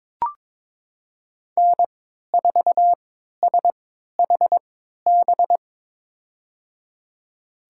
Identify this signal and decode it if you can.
Morse code, heard as a keyed tone, about 22 wpm, spaced as for 13 wpm, 700 Hz. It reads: N4SHB
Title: USA Call Signs - 3x Character Spacing - 22wpm